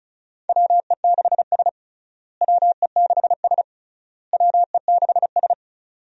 Morse code, heard as a keyed tone, about 36 wpm, 700 Hz. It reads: WE6S WE6S WE6S
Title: USA Call Signs - 4 Characters - Repeated 3x - 35wpm